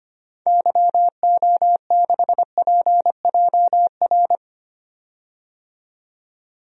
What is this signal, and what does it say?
Morse code, heard as a keyed tone, about 25 wpm, 700 Hz. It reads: YO6PJR
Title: DXCC-100 Call Signs - 6 Characters - 25wpm